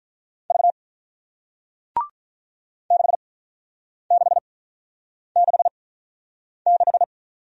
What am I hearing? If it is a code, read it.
Morse code, heard as a keyed tone, about 48 wpm, spaced as for 9 wpm, 700 Hz. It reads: V 6666